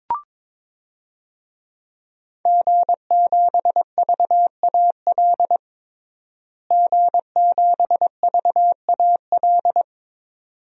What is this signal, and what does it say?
Morse code, heard as a keyed tone, about 22 wpm, 700 Hz. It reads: G7VAL G7VAL
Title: DXCC-100 Call Signs - 5 Characters - Repeated 2x - 22wpm